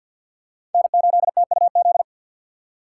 Morse code, heard as a keyed tone, about 51 wpm, 700 Hz. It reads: N8TUZ
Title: USA Call Signs - 4x Character Spacing - 50wpm